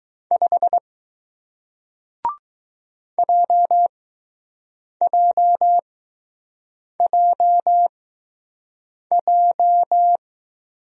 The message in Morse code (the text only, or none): 5 JJJJ